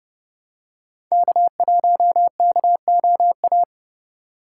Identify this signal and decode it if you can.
Morse code, heard as a keyed tone, about 29 wpm, 700 Hz. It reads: K1KOA